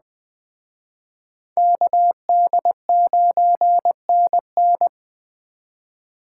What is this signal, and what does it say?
Morse code, heard as a keyed tone, about 21 wpm, 700 Hz. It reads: KD9NN